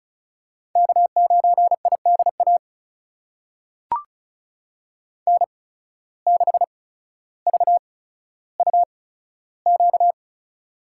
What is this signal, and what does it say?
Morse code, heard as a keyed tone, about 36 wpm, spaced as for 15 wpm, 700 Hz. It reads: K9IDA N6VUQ
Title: USA Call Signs - 5 Characters - 8x Character Spacing - 35wpm